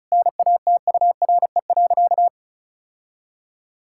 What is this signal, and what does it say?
Morse code, heard as a keyed tone, about 34 wpm, 700 Hz. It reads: NATURE.
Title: Sentences from Top 1000 Words - Review Part 5 - 35wpm